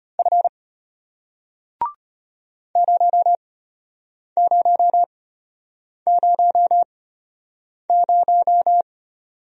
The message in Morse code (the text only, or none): F 0000